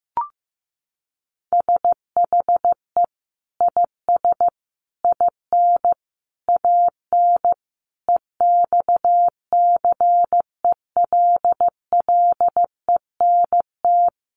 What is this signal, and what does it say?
Morse code, heard as a keyed tone, about 15 wpm, 700 Hz. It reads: SHE IS IN AN EXCELLENT
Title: Sentences from Top 1000 Words 15wpm